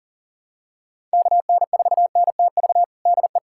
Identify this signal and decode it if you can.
Morse code, heard as a keyed tone, about 40 wpm, 700 Hz. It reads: KN4NTV DE
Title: USA Callsign DE Callsign 40wpm